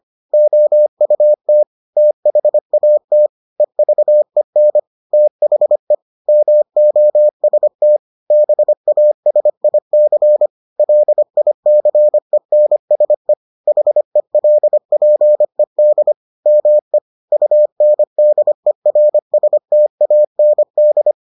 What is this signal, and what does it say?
Morse code, heard as a keyed tone, about 25 wpm, 600 Hz. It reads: OUT THAT EVEN THE MOST BASIC LICENSE HELPED ME UNDERSTAND